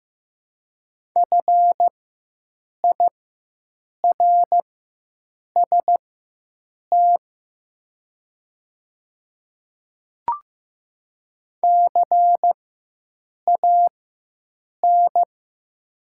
Morse code, heard as a keyed tone, about 15 wpm, spaced as for 7 wpm, 700 Hz. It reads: FIRST CAN
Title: 4x Character Spacing - Top 100 Words - 15wpm